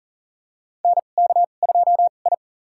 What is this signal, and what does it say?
Morse code, heard as a keyed tone, about 40 wpm, spaced as for 29 wpm, 700 Hz. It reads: NK2I
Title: USA Call Signs - 2x Character Spacing - 40wpm